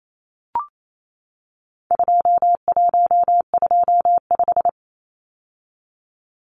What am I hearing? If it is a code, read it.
Morse code, heard as a keyed tone, about 28 wpm, 700 Hz. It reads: 2125
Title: Numbers - 4-Digit 28wpm